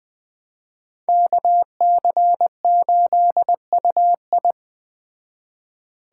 Morse code, heard as a keyed tone, about 20 wpm, 700 Hz. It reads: KC8UI
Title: Speed Racing - USA 2x2 Call Signs 17wpm